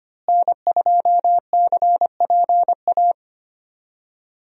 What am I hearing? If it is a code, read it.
Morse code, heard as a keyed tone, about 25 wpm, 700 Hz. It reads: N2CPA
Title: USA Call Signs - 5 Characters - Repeated 3x - 25wpm